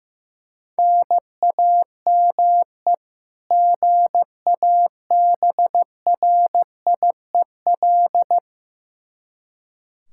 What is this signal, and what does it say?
Morse code, heard as a keyed tone, about 16 wpm, 700 Hz. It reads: NAME GABRIEL
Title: K1USN SST QSO Practice 15wpm